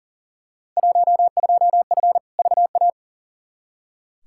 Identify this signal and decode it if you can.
Morse code, heard as a keyed tone, about 40 wpm, 700 Hz. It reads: 12F VA
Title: Field Day Exchanges Encore - 40wpm